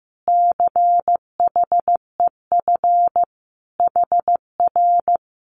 Morse code, heard as a keyed tone, about 15 wpm, 700 Hz. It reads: CHEF HR
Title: Common QSO Phrases Encore 15wpm